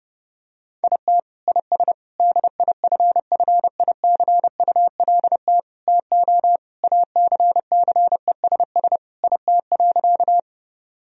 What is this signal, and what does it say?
Morse code, heard as a keyed tone, about 30 wpm, 700 Hz. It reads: IT IS DIFFICULT TO ACCESS IT.
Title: Sentences from Top 1000 Words - Review Part 4 - 30wpm